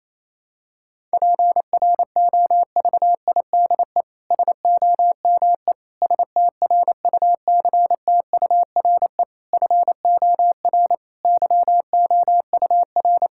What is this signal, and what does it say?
Morse code, heard as a keyed tone, about 28 wpm, 700 Hz. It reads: PROVIDE SOME STRUCTURE FOR YOUR
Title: Sentences from Top 1000 Words - Review Part 6 - 28wpm